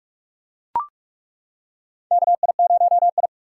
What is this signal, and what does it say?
Morse code, heard as a keyed tone, about 42 wpm, 700 Hz. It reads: KI0I
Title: K1USN SST QSO Practice - 45wpm